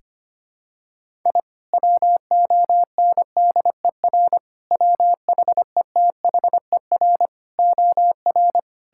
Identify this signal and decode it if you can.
Morse code, heard as a keyed tone, about 25 wpm, 700 Hz. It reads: I WONDER WHETHER OR
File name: practice / 25wpm / Sentences with 3-Word Phrases - 25wpm